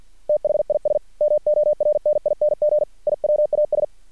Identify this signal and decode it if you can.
Morse code, heard as a keyed tone, about 50 wpm, 600 Hz. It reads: THIS MORNING I WAS
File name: archived / CRSnet